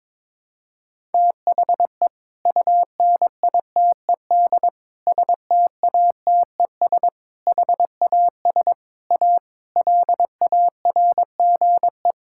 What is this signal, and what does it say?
Morse code, heard as a keyed tone, about 21 wpm, 700 Hz. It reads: THE UNITED STATES HAS A LARGE
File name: practice / 22wpm / Sentences with 4-Word Phrases - 22wpm